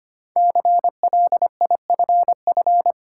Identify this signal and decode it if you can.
Morse code, heard as a keyed tone, about 25 wpm, 700 Hz. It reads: CLIFF
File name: 5 Letter Words - First 3 Letters Given - 25wpm